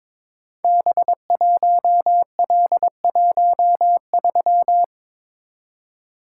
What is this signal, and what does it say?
Morse code, heard as a keyed tone, about 22 wpm, 700 Hz. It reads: B1L13
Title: Code Groups - 5 Letter:Numbers - 22wpm